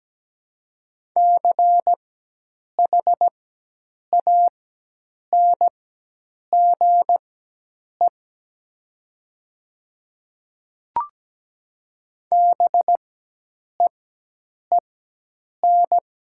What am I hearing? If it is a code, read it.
Morse code, heard as a keyed tone, about 17 wpm, spaced as for 8 wpm, 700 Hz. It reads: CHANGE BEEN
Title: 4x Character Spacing - Top 200 Words - 17wpm